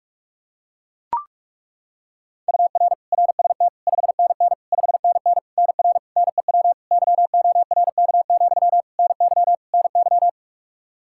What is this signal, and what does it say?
Morse code, heard as a keyed tone, about 43 wpm, 700 Hz. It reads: UR RST 5NN 5NN NR NEW YORK, NY NY
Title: K1USN SST QSO Practice 45wpm